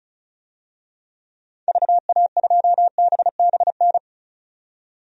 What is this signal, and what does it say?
Morse code, heard as a keyed tone, about 34 wpm, 700 Hz. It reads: VA2BBN
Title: DXCC-100 Call Signs - 6 Characters - 35wpm